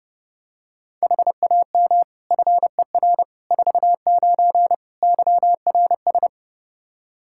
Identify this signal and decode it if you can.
Morse code, heard as a keyed tone, about 32 wpm, 700 Hz. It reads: HAM FER 49 YRS
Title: Practice QSO - Exchange 3 - More Info - 30wpm